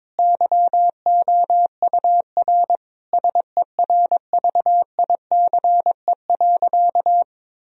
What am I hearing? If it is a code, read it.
Morse code, heard as a keyed tone, about 21 wpm, 700 Hz. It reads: YOUR SERVICE.
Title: Sentences from Top 1000 Words - Review Part 4 - 22wpm